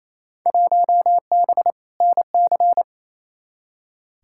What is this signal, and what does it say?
Morse code, heard as a keyed tone, about 28 wpm, 700 Hz. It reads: 1B NC
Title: Field Day Exchanges Encore - 28wpm